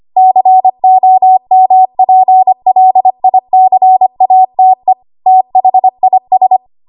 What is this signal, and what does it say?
Morse code, heard as a keyed tone, about 25 wpm, 750 Hz. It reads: COMPLICATE THIS
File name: Archive / 25 WPM / 25WPM